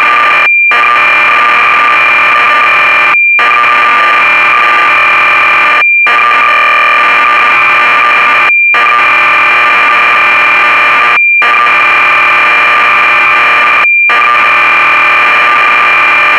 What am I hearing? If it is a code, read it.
Cassette data (Kansas City type).